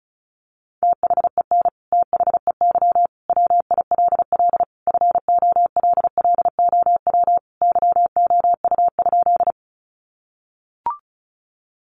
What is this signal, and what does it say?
Morse code, heard as a keyed tone, about 34 wpm, 700 Hz. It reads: THEN THEY WILL FOLLOW YOU?
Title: Sentences from Top 200 Words 35wpm